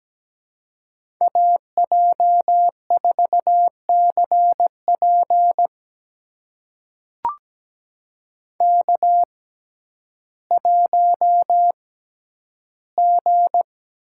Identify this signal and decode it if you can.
Morse code, heard as a keyed tone, about 17 wpm, spaced as for 10 wpm, 700 Hz. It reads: AJ4CP K1G
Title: USA Call Signs - 5 Characters - 6x Character Spacing - 17wpm